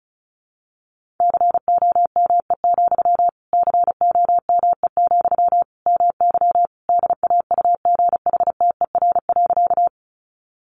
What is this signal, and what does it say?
Morse code, heard as a keyed tone, about 34 wpm, 700 Hz. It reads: COME, COME, MY DAUGHTER.
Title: Sentences from Top 1500 Words 35wpm